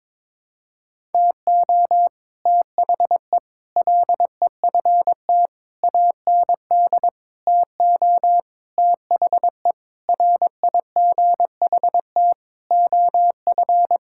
Morse code, heard as a keyed tone, about 22 wpm, 700 Hz. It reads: TO THE LEFT AND TO THE RIGHT OF
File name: Sets of 9 Words - 22wpm